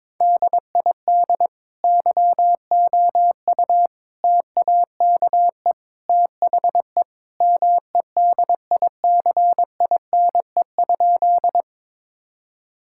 Morse code, heard as a keyed tone, about 21 wpm, 700 Hz. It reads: DID YOU TAKE THE MEDICINE?
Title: Sentences from Top 1000 Words - Review Part 6 - 22wpm